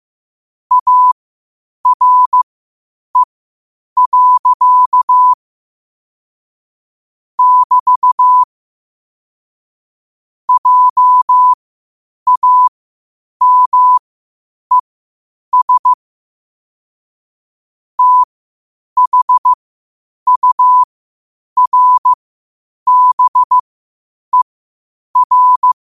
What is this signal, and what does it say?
Morse code, heard as a keyed tone, about 15 wpm, spaced as for 8 wpm, 1000 Hz. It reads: ARE. <BT> JAMES THURBER
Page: New quotes every day in morse code at 8 Words per minute.